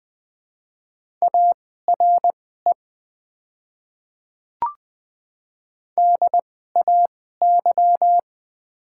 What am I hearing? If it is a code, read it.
Morse code, heard as a keyed tone, about 20 wpm, spaced as for 14 wpm, 700 Hz. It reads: ARE DAY
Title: 2x Character Spacing - Top 100 Words - 20wpm